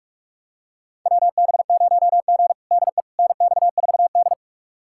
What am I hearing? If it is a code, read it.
Morse code, heard as a keyed tone, about 46 wpm, 700 Hz. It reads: WB0G DE NX4D
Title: USA Callsign DE Callsign - 4 Characters - 45wpm